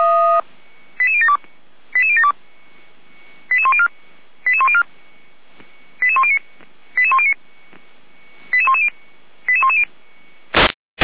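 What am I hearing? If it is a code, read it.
Five-tone selective call (ZVEI), heard as quick runs of five tones, tones of about 80 ms. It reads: 88072 88072 88104 88104 88109 88109 88110 88110
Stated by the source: ZVEI five-tone selective call received on SDR.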